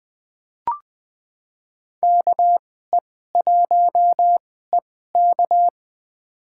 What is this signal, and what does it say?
Morse code, heard as a keyed tone, about 20 wpm, spaced as for 14 wpm, 700 Hz. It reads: KE1EK
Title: USA Call Signs - 5 Characters - 2x Character Spacing - 20wpm